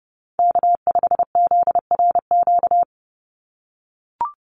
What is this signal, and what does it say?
Morse code, heard as a keyed tone, about 30 wpm, 700 Hz. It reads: K5ZRQ
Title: USA Call Signs 30wpm